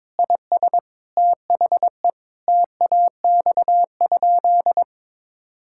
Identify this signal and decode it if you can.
Morse code, heard as a keyed tone, about 22 wpm, 700 Hz. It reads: IS THE TAX?
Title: Sentences from Top 1000 Words - Review Part 5 - 22wpm